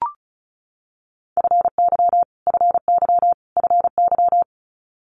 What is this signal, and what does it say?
Morse code, heard as a keyed tone, about 36 wpm, 700 Hz. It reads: FY FY FY
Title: Multi-letter Phonemes Repeated 3x 35wpm